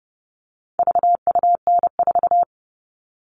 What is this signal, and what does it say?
Morse code, heard as a keyed tone, about 30 wpm, 700 Hz. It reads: VUN4
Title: All Cut Numbers - 4-Digit 30wpm